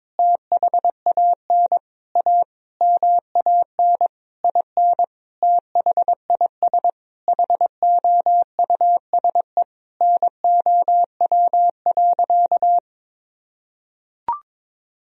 Morse code, heard as a keyed tone, about 22 wpm, 700 Hz. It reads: THAN A MAN IN THIS HOUSE NOW.
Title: Sentences from Top 1000 Words - Review Part 2 - 22wpm